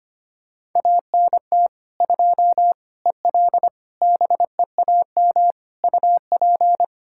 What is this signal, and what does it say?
Morse code, heard as a keyed tone, about 25 wpm, 700 Hz. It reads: ANT 2 EL BEAM UP
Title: Practice QSO - Exchange 2 - Info - 25wpm